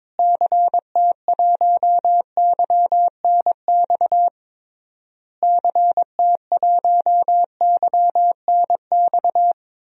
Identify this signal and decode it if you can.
Morse code, heard as a keyed tone, about 22 wpm, 700 Hz. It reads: CT1YNX CT1YNX
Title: DXCC-100 Call Signs - 6 Characters - Repeated 3x - 22wpm